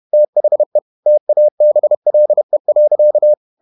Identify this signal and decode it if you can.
Morse code, heard as a keyed tone, about 31 wpm, 600 Hz. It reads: THE TABLE.